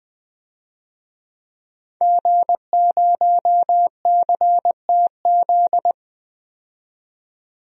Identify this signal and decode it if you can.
Morse code, heard as a keyed tone, about 20 wpm, 700 Hz. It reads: G0CTZ
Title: DXCC-100 Call Signs - 5 Characters - 20wpm